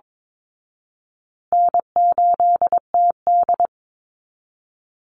Morse code, heard as a keyed tone, about 22 wpm, 700 Hz. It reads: N8TD
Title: Easy Call Signs Repeated 3x - 22wpm